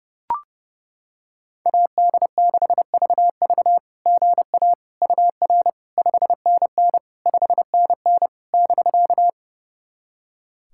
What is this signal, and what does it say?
Morse code, heard as a keyed tone, about 32 wpm, 690 Hz. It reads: AD6VV GA UR 5NN 5NN <BK>
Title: SOTA QSO Practice 30wpm